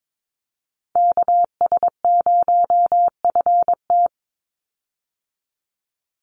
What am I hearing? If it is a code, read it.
Morse code, heard as a keyed tone, about 22 wpm, 700 Hz. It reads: KS0FT
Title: USA Call Signs 22wpm